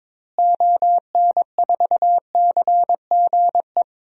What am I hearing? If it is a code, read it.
Morse code, heard as a keyed tone, about 22 wpm, 700 Hz. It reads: ON4CGE
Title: DXCC-100 Call Signs - 6 Characters - Repeated 2x - 22wpm